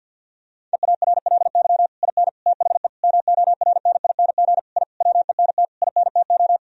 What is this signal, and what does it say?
Morse code, heard as a keyed tone, about 50 wpm, 700 Hz. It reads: EARLY IN THE MORNING I WENT INTO
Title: Sentences from Top 1000 Words - Review Part 2 - 50wpm